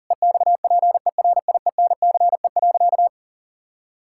Morse code, heard as a keyed tone, about 37 wpm, 700 Hz. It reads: EXPERIENCE.
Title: Sentences from Top 1600 Words 40wpm